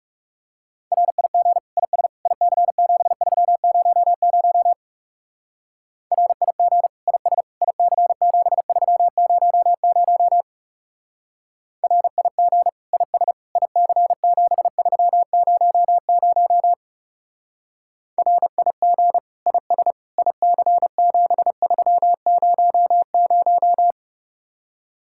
Morse code, heard as a keyed tone, about 36 wpm, 700 Hz. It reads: RIG IS IC7300 RIG IS IC7300 RIG IS IC7300 RIG IS IC7300
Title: Speed-Racing Common QSO Phrases - 30wpm